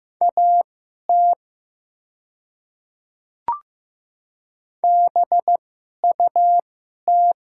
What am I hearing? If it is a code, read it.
Morse code, heard as a keyed tone, about 15 wpm, spaced as for 11 wpm, 700 Hz. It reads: AT BUT
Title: 2x Character Spacing - Top 300 Words - 15wpm